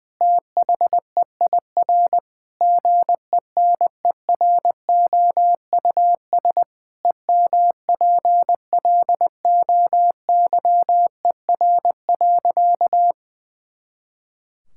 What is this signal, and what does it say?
Morse code, heard as a keyed tone, about 20 wpm, 700 Hz. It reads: THEIR GENEROUS EMPLOYER.
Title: Pangram Challenge v2 - 20wpm